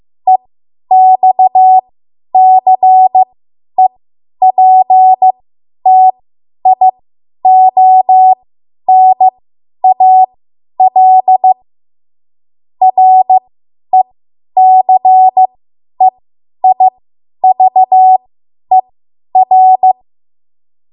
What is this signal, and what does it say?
Morse code, heard as a keyed tone, about 15 wpm, spaced as for 10 wpm, 750 Hz. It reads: EXCEPTIONAL RECEIVER
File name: Morse / 10WPM